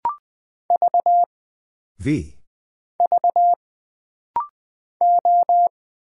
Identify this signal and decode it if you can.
Morse code, heard as a keyed tone, about 20 wpm, 700 Hz. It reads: V V O
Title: Single Character (Letter:Number:Punctuation:Prosign) - Rapid-Fire - 20wpm